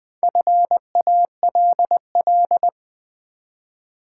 Morse code, heard as a keyed tone, about 20 wpm, 700 Hz. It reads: FALL
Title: Build A Word - 1 Letter At A Time - 4-Letter Words - 20wpm